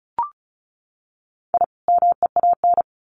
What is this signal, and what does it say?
Morse code, heard as a keyed tone, about 34 wpm, 700 Hz. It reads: I MEAN
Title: Sets of 2 Words 35wpm